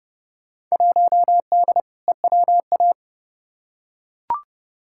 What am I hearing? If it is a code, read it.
Morse code, heard as a keyed tone, about 30 wpm, 700 Hz. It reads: 1D EWA
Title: Field Day Exchanges - 30wpm